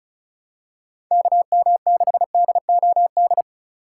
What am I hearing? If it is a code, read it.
Morse code, heard as a keyed tone, about 34 wpm, 700 Hz. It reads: KM6DOD
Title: USA Call Signs - 2x Character Spacing - 35wpm